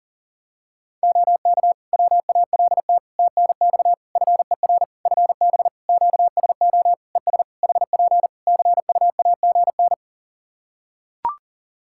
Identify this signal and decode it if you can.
Morse code, heard as a keyed tone, about 40 wpm, 700 Hz. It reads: OK WALT TNX FER FB QSO ES HP CUAGN
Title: Practice QSO - Exchange 4 - Ending - 40wpm